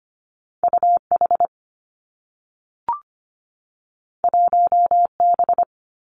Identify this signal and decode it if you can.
Morse code, heard as a keyed tone, about 25 wpm, 700 Hz. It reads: UH 1B
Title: Two Challenging Letter-Number Combinations 25wpm